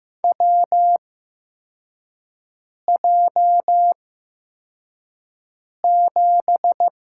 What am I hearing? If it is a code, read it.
Morse code, heard as a keyed tone, about 15 wpm, spaced as for 4 wpm, 700 Hz. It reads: WJ7
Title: USA Call Signs - 4 Characters - 8x Character Spacing - 15wpm